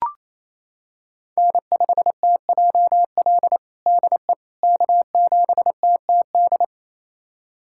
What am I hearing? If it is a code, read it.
Morse code, heard as a keyed tone, about 28 wpm, 700 Hz. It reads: N5TJL DE K7TTD
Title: USA Callsign DE Callsign - 5 Characters - 28wpm